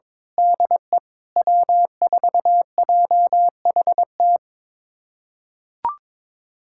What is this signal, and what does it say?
Morse code, heard as a keyed tone, about 22 wpm, 700 Hz. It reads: DE W4JHT
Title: USA Callsign DE Callsign 22wpm